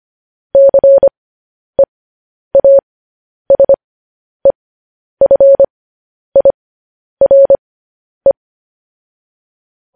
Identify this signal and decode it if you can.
Morse code, heard as a keyed tone, about 25 wpm, spaced as for 10 wpm, 550 Hz. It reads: CEASEFIRE
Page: news-headlines-in-morse-code-at-10-wpm